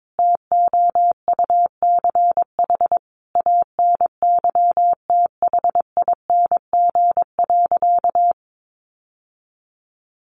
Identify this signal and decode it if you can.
Morse code, heard as a keyed tone, about 22 wpm, 700 Hz. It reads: TOUCH ANYTHING.
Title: Sentences from Top 1200 Words 22wpm